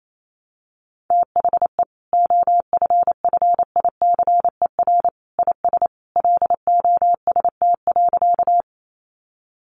Morse code, heard as a keyed tone, about 28 wpm, 700 Hz. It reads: THE OFFICER IS LOST.
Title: Sentences from Top 1400 Words 28wpm